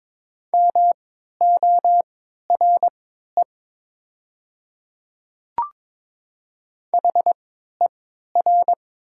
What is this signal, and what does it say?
Morse code, heard as a keyed tone, about 22 wpm, spaced as for 13 wpm, 700 Hz. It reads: MORE HER
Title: 3x Character Spacing - Top 300 Words - 22wpm